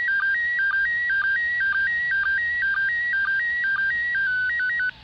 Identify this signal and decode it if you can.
SSTV (slow-scan television), PD 120.
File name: spazio / ISS 145 800MHz SSTV 5